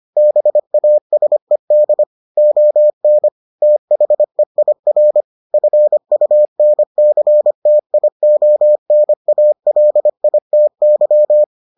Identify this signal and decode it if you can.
Morse code, heard as a keyed tone, about 25 wpm, 600 Hz. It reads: BASED ON THEIR FUNCTIONALITY